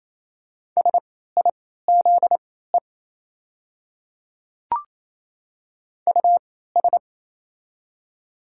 Morse code, heard as a keyed tone, about 28 wpm, spaced as for 16 wpm, 690 Hz. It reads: SIZE US
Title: 3x Character Spacing - Top 500 Words - 28wpm